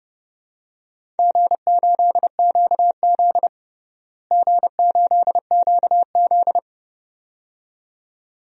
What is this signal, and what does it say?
Morse code, heard as a keyed tone, about 30 wpm, 700 Hz. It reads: G8QZ G8QZ
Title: DXCC-100 Call Signs - 4 Characters - Repeated 2x - 30wpm